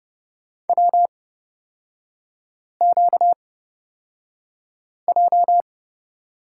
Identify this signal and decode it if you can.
Morse code, heard as a keyed tone, about 29 wpm, spaced as for 5 wpm, 690 Hz. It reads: WQJ
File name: course / 30:10wpm / 236 Lesson 38e - Letter q - Three Characters 30wpm